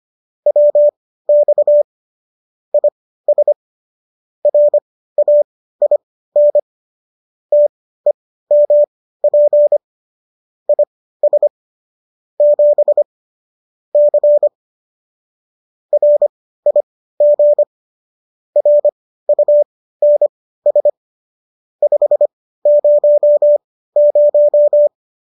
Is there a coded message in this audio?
Yes: Morse code from 0.5 to 24.9 s, heard as a keyed tone, about 25 wpm, spaced as for 15 wpm, 590 Hz. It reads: WX IS RAIN TEMP IS 7 C RIG RUNS 500